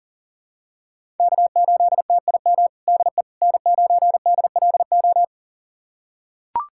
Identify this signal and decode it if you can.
Morse code, heard as a keyed tone, about 40 wpm, 700 Hz. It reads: K8TIM DE N9DLO
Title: USA Callsign DE Callsign - 5 Characters - 40wpm